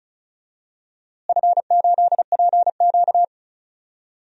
Morse code, heard as a keyed tone, about 34 wpm, 700 Hz. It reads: F8PQ
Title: DXCC-100 Call Signs - 4 Characters - Repeated 3x - 35wpm